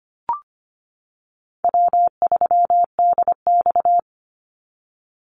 Morse code, heard as a keyed tone, about 25 wpm, 700 Hz. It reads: W3DX
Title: NA SOTA Activator Callsigns 25wpm